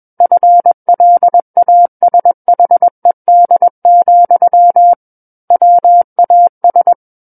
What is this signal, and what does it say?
Morse code, heard as a keyed tone, about 21 wpm, 700 Hz. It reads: FLASHED, WAS
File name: Book One / 12-Chapter 12 - 21 WPM